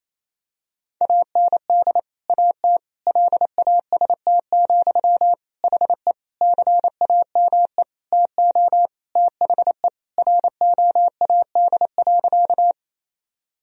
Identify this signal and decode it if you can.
Morse code, heard as a keyed tone, about 28 wpm, 700 Hz. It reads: AND AT LAST, HE CAME TO THE ROAD.
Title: Sentences from Top 1000 Words - Review Part 2 - 28wpm